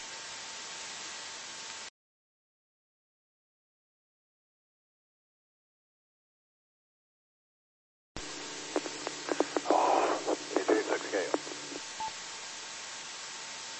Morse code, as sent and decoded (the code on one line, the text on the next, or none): none